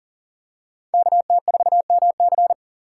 Morse code, heard as a keyed tone, about 40 wpm, 700 Hz. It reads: KT4MC
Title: USA Call Signs - 5 Characters - 4x Character Spacing - 40wpm